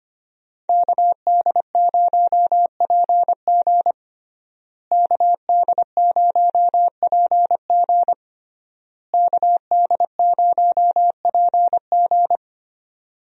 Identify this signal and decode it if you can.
Morse code, heard as a keyed tone, about 25 wpm, 700 Hz. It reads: KD0PG KD0PG KD0PG
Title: USA Call Signs - 5 Characters - Repeated 3x - 25wpm